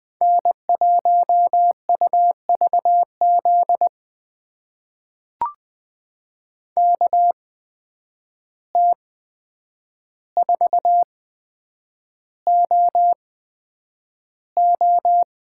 Morse code, heard as a keyed tone, about 20 wpm, spaced as for 9 wpm, 700 Hz. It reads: N1UVZ KT4OO
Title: USA Call Signs - 5 Characters - 8x Character Spacing - 20wpm